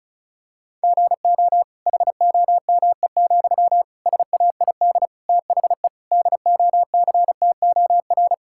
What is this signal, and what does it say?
Morse code, heard as a keyed tone, about 36 wpm, 700 Hz. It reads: GO HOME, SAID THE DOCTOR
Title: Sentences from Top 1600 Words 35wpm